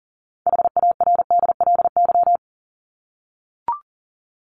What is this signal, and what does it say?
Morse code, heard as a keyed tone, about 40 wpm, 700 Hz. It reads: HARDLY
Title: LY Suffix - Top 100 Words with Base Words 40wpm